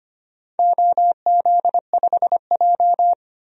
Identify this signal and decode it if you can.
Morse code, heard as a keyed tone, about 25 wpm, 700 Hz. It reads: OZ5J
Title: DXCC-100 Call Signs - 4 Characters - 25wpm